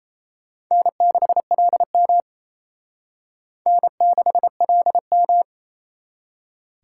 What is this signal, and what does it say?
Morse code, heard as a keyed tone, about 36 wpm, 700 Hz. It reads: N6LM N6LM